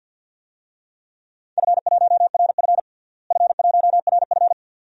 Morse code, heard as a keyed tone, about 47 wpm, 700 Hz. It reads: F1RF F1RF
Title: DXCC-100 Call Signs - 4 Characters - Repeated 2x - 50wpm